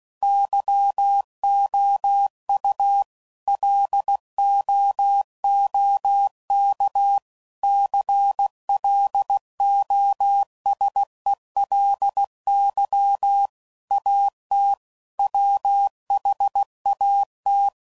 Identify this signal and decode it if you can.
Morse code, heard as a keyed tone, about 16 wpm, 800 Hz. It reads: YOU LOOK CLOSELY AT WHAT
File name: CW Code Groups @15WPM / Technical Subjects / OR Gate